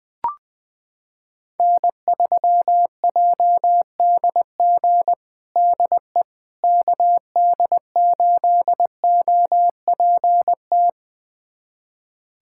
Morse code, heard as a keyed tone, about 20 wpm, 700 Hz. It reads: N3JDG DE KD8OPT
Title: USA Callsign DE Callsign 20wpm